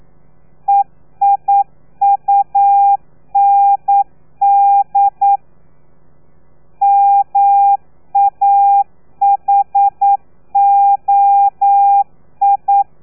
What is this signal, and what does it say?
Morse code, heard as a keyed tone, about 9 wpm, 800 Hz. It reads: EIUND MAHOI
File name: telegrafia / i2tfj / Lez3 40